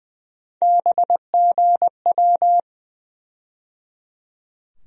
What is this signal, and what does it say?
Morse code, heard as a keyed tone, about 20 wpm, 700 Hz. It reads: BGW
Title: USA Call Sign Postfixes - 3 Letters 20wpm